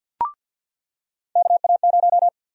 Morse code, heard as a keyed tone, about 53 wpm, 700 Hz. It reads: KA0
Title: USA Call Sign Prefixes - Repeated 2x - 50wpm